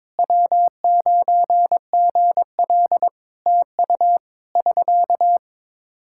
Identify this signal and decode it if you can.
Morse code, heard as a keyed tone, about 22 wpm, 700 Hz. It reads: W9GL TU <SK>
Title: Practice QSO - Exchange 4 - Ending - 22wpm